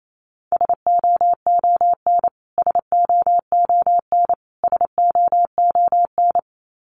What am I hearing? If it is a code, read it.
Morse code, heard as a keyed tone, about 28 wpm, 700 Hz. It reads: SOON SOON SOON
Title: Top 500 Words Repeated 28wpm